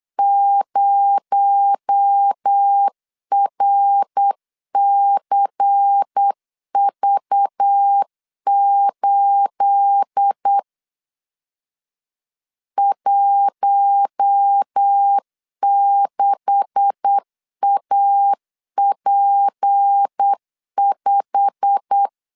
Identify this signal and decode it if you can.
Morse code, heard as a keyed tone, about 8 wpm, 800 Hz. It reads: 0RCV8 16AP5
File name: LEZIONI / 10 SIGNS / 10A SLASH 8wpm 3452